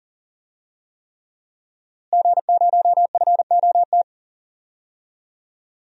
Morse code, heard as a keyed tone, about 39 wpm, 700 Hz. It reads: G0FOT
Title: DXCC-100 Call Signs - 5 Characters - Repeated 3x - 40wpm